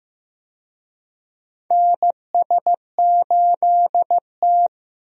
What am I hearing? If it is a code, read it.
Morse code, heard as a keyed tone, about 15 wpm, 700 Hz. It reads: NS8T
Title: USA Call Signs - 4 Characters - Repeated 2x - 15wpm